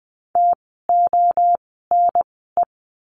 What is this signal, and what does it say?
Morse code, heard as a keyed tone, about 20 wpm, spaced as for 15 wpm, 700 Hz. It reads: TONE
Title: 068 Lesson 12f - Letter d - Words 20wpm